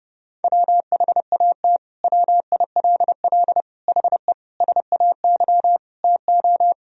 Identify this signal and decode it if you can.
Morse code, heard as a keyed tone, about 30 wpm, 700 Hz. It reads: WHAT WILL HE SAY TO
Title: Sentences from Top 1000 Words - Review Part 3 - 30wpm